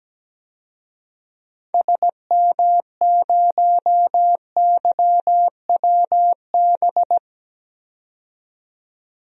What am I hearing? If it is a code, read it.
Morse code, heard as a keyed tone, about 17 wpm, 700 Hz. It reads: SM0YWB
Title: DXCC-100 Call Signs - 6 Characters - 17wpm